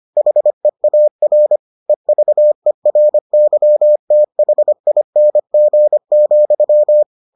Morse code, heard as a keyed tone, about 25 wpm, 600 Hz. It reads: HEAR EVERYTHING,